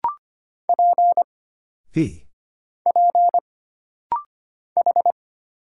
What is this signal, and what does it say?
Morse code, heard as a keyed tone, about 25 wpm, 700 Hz. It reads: P P H
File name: Single Character (Letter:Number:Punctuation:Prosign) - Rapid-Fire - 25wpm